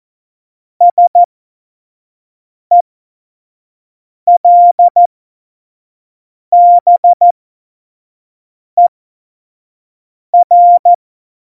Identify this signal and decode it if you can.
Morse code, heard as a keyed tone, about 14 wpm, spaced as for 5 wpm, 700 Hz. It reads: SELBER
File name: Morsekurs / Witze / Witz 11. Juni 2019, 25 F70